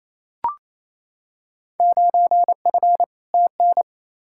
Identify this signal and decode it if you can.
Morse code, heard as a keyed tone, about 28 wpm, 700 Hz. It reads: 9F TN
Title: Field Day Exchanges Encore - 28wpm